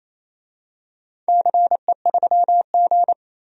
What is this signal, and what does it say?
Morse code, heard as a keyed tone, about 28 wpm, 700 Hz. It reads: CE3G
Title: DXCC-100 Call Signs - 4 Characters - 28wpm